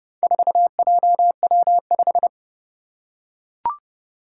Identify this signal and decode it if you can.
Morse code, heard as a keyed tone, about 30 wpm, 700 Hz. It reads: 4JW5
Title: Four Challenging Letter-Number Combinations 30wpm